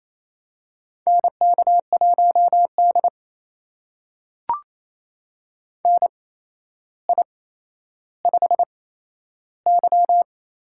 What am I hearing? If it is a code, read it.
Morse code, heard as a keyed tone, about 28 wpm, spaced as for 12 wpm, 700 Hz. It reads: NK1D NI5Y
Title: USA Call Signs - 4 Characters - 8x Character Spacing - 28wpm